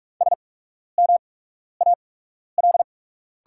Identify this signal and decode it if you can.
Morse code, heard as a keyed tone, about 43 wpm, spaced as for 12 wpm, 690 Hz. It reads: SMAL
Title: 8x Character Spacing - Top 500 Words - 45wpm